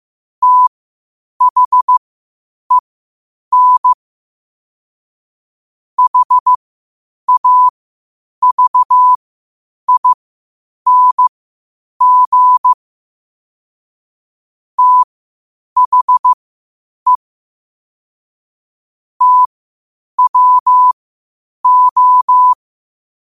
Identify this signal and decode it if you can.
Morse code, heard as a keyed tone, about 15 wpm, spaced as for 8 wpm, 1000 Hz. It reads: THEN HAVING THE TWO
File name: qotdfiles / 8wpm